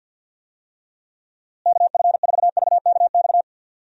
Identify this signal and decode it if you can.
Morse code, heard as a keyed tone, about 50 wpm, 700 Hz. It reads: KF4VKX
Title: USA Call Signs - 6 Characters - Repeated 2x - 50wpm